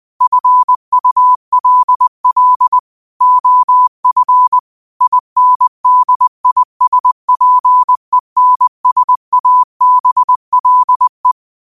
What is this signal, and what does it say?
Morse code, heard as a keyed tone, about 20 wpm, 1000 Hz. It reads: FULL OF INDISPENSABLE